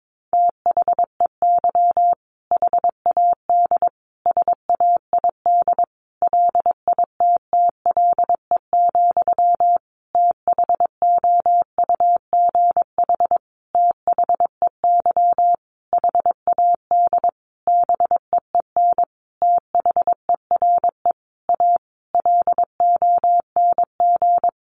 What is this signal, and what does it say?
Morse code, heard as a keyed tone, about 22 wpm, 700 Hz. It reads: THEY HAD SAID LITTLE, THOUGH THEY HAD BEEN THERE A LONG
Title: Sentences from Top 400 Words 22wpm